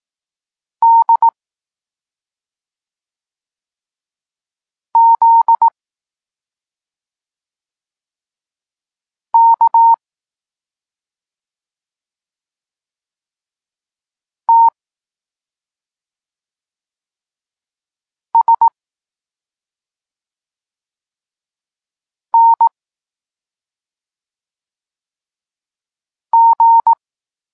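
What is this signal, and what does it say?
Morse code, heard as a keyed tone, about 18 wpm, 920 Hz. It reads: D Z K T S N G
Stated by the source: Morse code lesson for learning the letters K and X, from the SP9MOA Niepołomicki Klub Krótkofalowców course.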